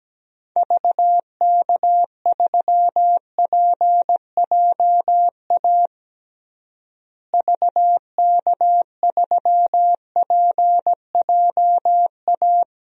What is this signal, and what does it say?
Morse code, heard as a keyed tone, about 17 wpm, 700 Hz. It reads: VK3PJA VK3PJA
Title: DXCC-100 Call Signs - 6 Characters - Repeated 2x - 17wpm